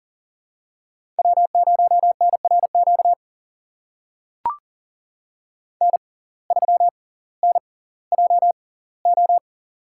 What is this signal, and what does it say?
Morse code, heard as a keyed tone, about 40 wpm, spaced as for 20 wpm, 700 Hz. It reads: W0NRQ N3NJO
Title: USA Call Signs - 5 Characters - 6x Character Spacing - 40wpm